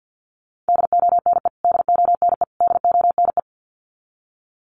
Morse code, heard as a keyed tone, about 48 wpm, 700 Hz. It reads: DONE DONE DONE
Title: Top 500 Words Repeated 50wpm